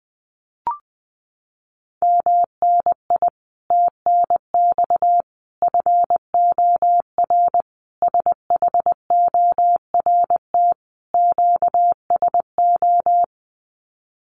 Morse code, heard as a keyed tone, about 20 wpm, 700 Hz. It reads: MNI TNX FOR SHORT QSO
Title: Common QSO Phrases 20wpm